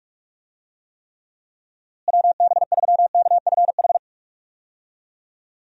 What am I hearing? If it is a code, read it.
Morse code, heard as a keyed tone, about 46 wpm, 700 Hz. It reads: WB3KFH
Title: USA Call Signs - 6 Characters - Repeated 2x - 45wpm